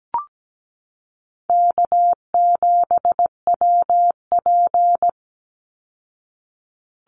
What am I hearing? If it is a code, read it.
Morse code, heard as a keyed tone, about 17 wpm, 700 Hz. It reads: K7WP
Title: Easy Call Signs Repeated 2x - 17wpm